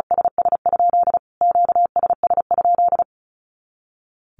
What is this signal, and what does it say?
Morse code, heard as a keyed tone, about 34 wpm, 700 Hz. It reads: SS? QSS?
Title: Q-Codes Repeated 3x 35wpm